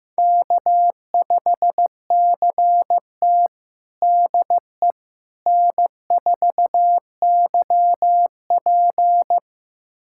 Morse code, heard as a keyed tone, about 15 wpm, 700 Hz. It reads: K5CT DE N4YP
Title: USA Callsign DE Callsign - 4 Characters - 15wpm